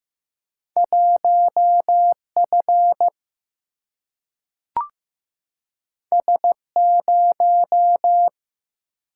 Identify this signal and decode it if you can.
Morse code, heard as a keyed tone, about 15 wpm, 700 Hz. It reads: 1 F S 0
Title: Single-Letter Single-Number 15wpm